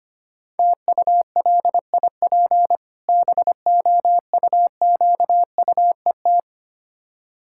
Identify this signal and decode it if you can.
Morse code, heard as a keyed tone, about 25 wpm, 690 Hz. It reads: TULIP BOUQUET
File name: practice / 25wpm / Difficult Words with Hints - 25wpm